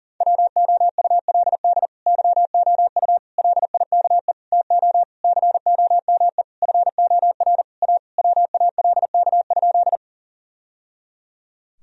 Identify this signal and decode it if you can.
Morse code, heard as a keyed tone, about 40 wpm, 700 Hz. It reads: WOULD YOU LIKE TO COME FOR A WALK?
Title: Sentences from Top 1000 Words - Review Part 1 - 40wpm